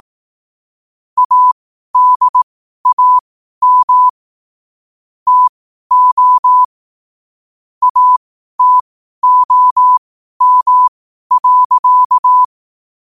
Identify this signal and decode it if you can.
Morse code, heard as a keyed tone, about 18 wpm, spaced as for 12 wpm, 1000 Hz. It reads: ADAM TO ATOM.
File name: qotdfiles / 12wpm